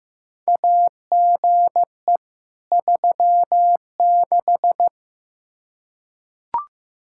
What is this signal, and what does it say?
Morse code, heard as a keyed tone, about 15 wpm, 700 Hz. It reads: AGE 36
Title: Practice QSO - Exchange 3 - More Info - 15wpm